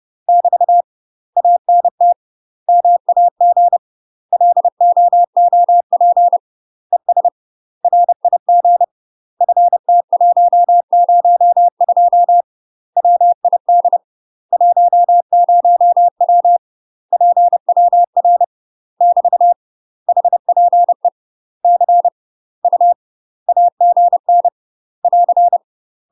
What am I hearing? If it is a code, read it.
Morse code, heard as a keyed tone, about 30 wpm, 700 Hz. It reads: <BT> ANT MAG LOOP ES RIG FT102 WID 10W PWR <BT> HPE C U AGN <AR>
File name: UK MASTER QSOs TRAINING / 30WPM / UK MASTER QSOs 0011